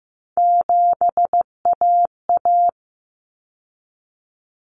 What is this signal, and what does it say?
Morse code, heard as a keyed tone, about 15 wpm, 700 Hz. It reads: 7AA
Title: Common Cut Numbers - 3-Digit 15wpm